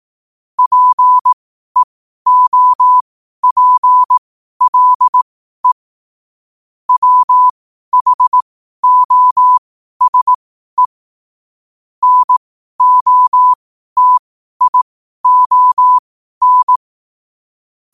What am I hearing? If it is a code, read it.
Morse code, heard as a keyed tone, about 18 wpm, spaced as for 12 wpm, 1000 Hz. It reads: PEOPLE WHOSE NOTION
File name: qotdfiles / 12wpm